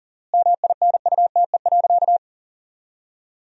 Morse code, heard as a keyed tone, about 40 wpm, 700 Hz. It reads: MINUTE.
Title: 4x Word-Spacing - Sentences from Top 500 Words - 40wpm